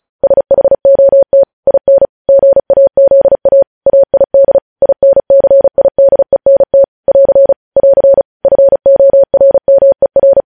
Morse code, heard as a keyed tone, about 36 wpm, 550 Hz. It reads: SHOT IN GAZA AID INCIDENT <AR> <AR> FORMER
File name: podcast / News headlines-1x-35x35